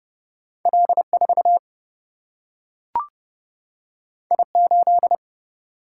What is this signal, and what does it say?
Morse code, heard as a keyed tone, about 30 wpm, 700 Hz. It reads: L 4 I 8
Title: Single-Letter Single-Number 30wpm